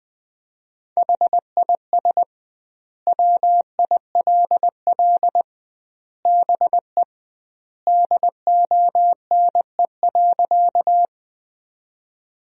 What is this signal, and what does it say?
Morse code, heard as a keyed tone, about 20 wpm, 700 Hz. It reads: HIS WILL BE DONE.
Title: 2x Word-Spacing - Sentences from Top 500 Words - 20wpm